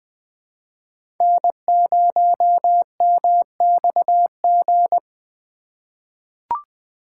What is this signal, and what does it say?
Morse code, heard as a keyed tone, about 20 wpm, 700 Hz. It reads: N0MXG
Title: USA Call Signs - 5 Characters - 2x Character Spacing - 20wpm